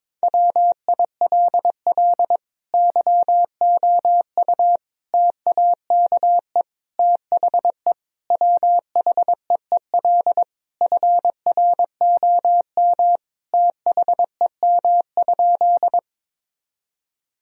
Morse code, heard as a keyed tone, about 22 wpm, 700 Hz. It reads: WILL YOU TAKE THE WHEEL FROM THEM?
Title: Sentences from Top 1000 Words - Review Part 3 - 22wpm